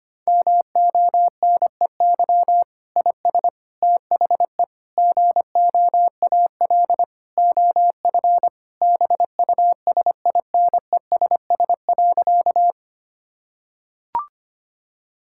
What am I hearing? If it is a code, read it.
Morse code, heard as a keyed tone, about 25 wpm, 700 Hz. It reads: MONEY IS THE GOAL OF BUSINESS.
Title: Sentences from Top 1000 Words - Review Part 5 - 25wpm